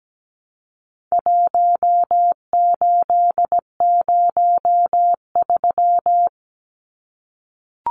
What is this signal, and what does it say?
Morse code, heard as a keyed tone, about 17 wpm, 700 Hz. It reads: 1803
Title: Numbers - 4-Digit 17wpm